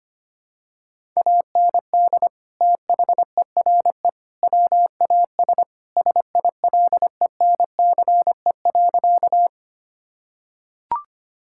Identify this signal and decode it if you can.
Morse code, heard as a keyed tone, about 25 wpm, 700 Hz. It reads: AND THERE WAS SILENCE.
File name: practice / 25wpm / Sentences with 3-Word Phrases - 25wpm